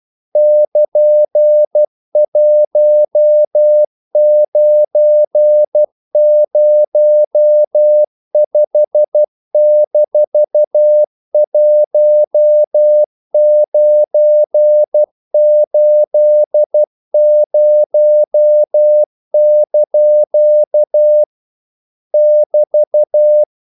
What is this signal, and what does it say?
Morse code, heard as a keyed tone, about 12 wpm, 600 Hz. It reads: <KN>1905-1980) <BT>
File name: CW / Practice / EnglishQuotations0312